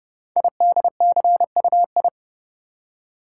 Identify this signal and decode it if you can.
Morse code, heard as a keyed tone, about 30 wpm, 700 Hz. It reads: IDCUI
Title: Code Groups - 5 Letter:Numbers - 30wpm